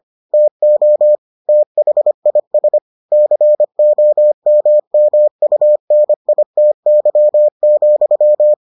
Morse code, heard as a keyed tone, about 25 wpm, 600 Hz. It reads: TO THIS COMMUNITY,